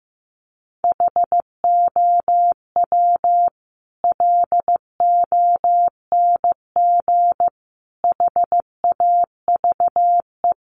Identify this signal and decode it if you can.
Morse code, heard as a keyed tone, about 15 wpm, 700 Hz. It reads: HOW LONG HAVE
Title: Sentences from Top 200 Words 15wpm